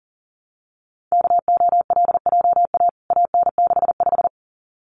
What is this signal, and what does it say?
Morse code, heard as a keyed tone, about 40 wpm, 700 Hz. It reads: KOLJA AN65
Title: CWT Exchange 40wpm